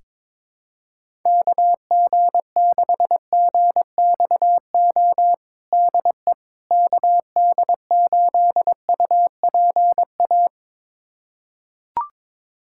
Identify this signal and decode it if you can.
Morse code, heard as a keyed tone, about 22 wpm, 700 Hz. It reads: KG6GXO DE KD8UPA
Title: USA Callsign DE Callsign - 6 Characters - 22wpm